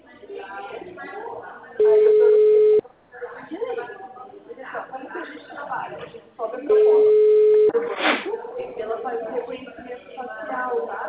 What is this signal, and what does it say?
Ringback tone (Europe, 425 Hz), with speech.